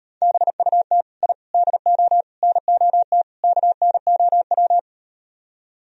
Morse code, heard as a keyed tone, about 38 wpm, 700 Hz. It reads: BUT I DO NOT KNOW